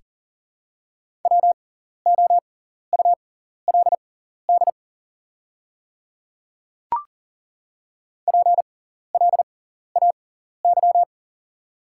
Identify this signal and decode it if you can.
Morse code, heard as a keyed tone, about 40 wpm, spaced as for 14 wpm, 690 Hz. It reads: WOULD PLAY
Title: 6x Character Spacing - Top 500 Words - 40wpm